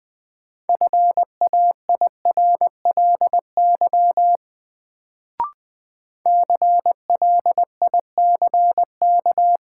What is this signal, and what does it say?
Morse code, heard as a keyed tone, about 20 wpm, 700 Hz. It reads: FAIRLY CLICK
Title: Sentences from Top 1100 Words - New Word Drill Alt - 20wpm